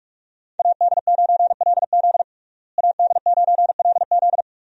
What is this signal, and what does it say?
Morse code, heard as a keyed tone, about 41 wpm, 700 Hz. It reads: AD9LZ AD9LZ
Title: USA Call Signs - 5 Characters - Repeated 3x - 45wpm